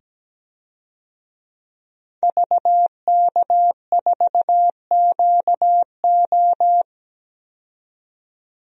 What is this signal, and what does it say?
Morse code, heard as a keyed tone, about 17 wpm, 700 Hz. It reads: VK4QO
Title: DXCC-100 Call Signs - 5 Characters - 17wpm